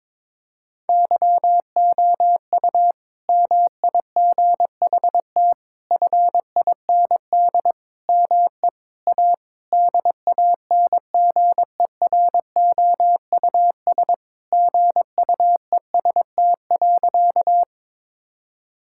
Morse code, heard as a keyed tone, about 22 wpm, 700 Hz. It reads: YOU MIGHT FIND ME A DANGEROUS GUEST.
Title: Sentences from Top 1600 Words 22wpm